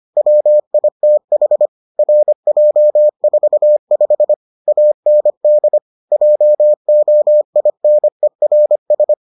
Morse code, heard as a keyed tone, about 25 wpm, 600 Hz. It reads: WITH RJ45 AND JOINERS